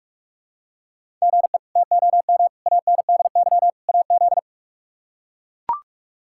Morse code, heard as a keyed tone, about 44 wpm, 700 Hz. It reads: GE TOM ANDY AZ
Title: K1USN SST QSO Practice - 45wpm